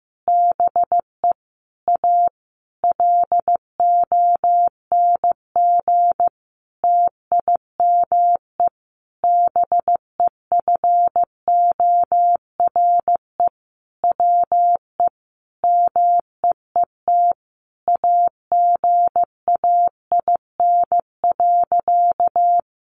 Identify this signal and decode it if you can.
Morse code, heard as a keyed tone, about 15 wpm, 700 Hz. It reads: BE A LONG TIME BEFORE WE MEET AGAIN.
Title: Sentences from Top 1100 Words 15wpm